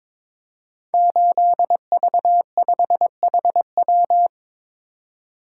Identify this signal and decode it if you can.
Morse code, heard as a keyed tone, about 22 wpm, 700 Hz. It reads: 8V5HW
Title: Code Groups - 5 Letter:Numbers - 22wpm